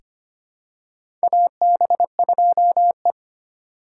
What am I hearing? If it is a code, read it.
Morse code, heard as a keyed tone, about 25 wpm, 690 Hz. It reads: AB2E
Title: USA Call Signs - 4 Characters - 4x Character Spacing - 25wpm